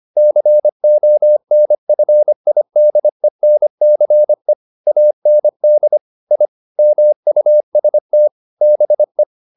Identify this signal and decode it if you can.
Morse code, heard as a keyed tone, about 25 wpm, 600 Hz. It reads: CONFIDENCE AND I MUST BE
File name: the old man and the sea part 1 25wpm